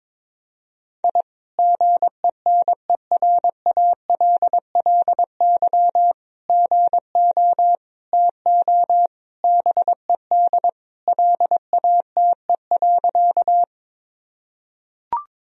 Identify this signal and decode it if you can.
Morse code, heard as a keyed tone, about 22 wpm, 700 Hz. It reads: I GENERALLY GO TO BED LATE.
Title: Sentences from Top 1000 Words - Review Part 4 - 22wpm